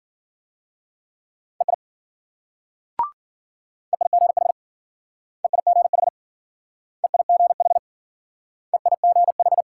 Morse code, heard as a keyed tone, about 50 wpm, 700 Hz. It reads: EI EIGH EIGH EIGH EIGH
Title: Speed-Racing Multi-letter Phonemes - 40wpm